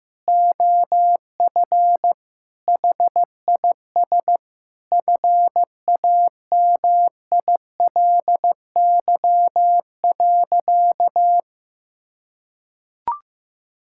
Morse code, heard as a keyed tone, about 15 wpm, 700 Hz. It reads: OF HIS FAMILY.
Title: Sentences from Top 1000 Words - Review Part 5 - 15wpm